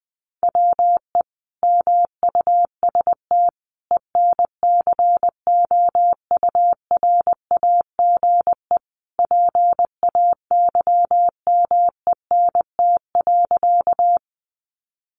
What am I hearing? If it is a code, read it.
Morse code, heard as a keyed tone, about 19 wpm, 700 Hz. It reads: WE MUST ENCOURAGE PAYMENT.
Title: Sentences from Top 1400 Words 20wpm